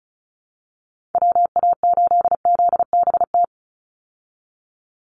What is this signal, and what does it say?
Morse code, heard as a keyed tone, about 34 wpm, 700 Hz. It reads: WA8ZBT
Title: Call Signs Repeated 3x - 35wpm